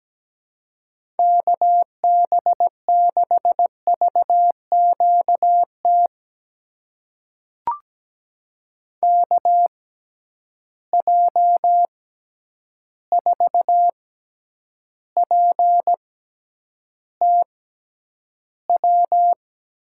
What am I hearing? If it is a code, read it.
Morse code, heard as a keyed tone, about 17 wpm, spaced as for 9 wpm, 700 Hz. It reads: KB6VQT KJ4PTW
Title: USA Call Signs - 6 Characters - 6x Character Spacing - 17wpm